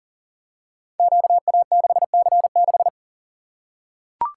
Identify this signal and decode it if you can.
Morse code, heard as a keyed tone, about 40 wpm, 700 Hz. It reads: QA6C6
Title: Code Groups - 5 Letter:Numbers - 40wpm